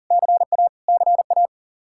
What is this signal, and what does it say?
Morse code, heard as a keyed tone, about 40 wpm, 700 Hz. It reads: CA CA
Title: K1USN SST QSO Practice 40wpm